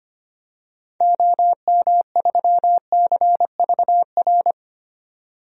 Morse code, heard as a keyed tone, about 25 wpm, 700 Hz. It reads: OM3CVR
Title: DXCC-100 Call Signs - 6 Characters - 25wpm